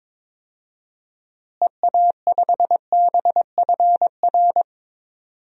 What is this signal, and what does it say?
Morse code, heard as a keyed tone, about 22 wpm, 700 Hz. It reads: EA5BFR
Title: DXCC-100 Call Signs - 6 Characters - Repeated 3x - 22wpm